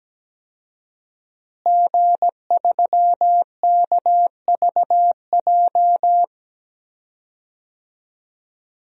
Morse code, heard as a keyed tone, about 17 wpm, 700 Hz. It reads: G3KVJ
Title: DXCC-100 Call Signs - 5 Characters - Repeated 2x - 17wpm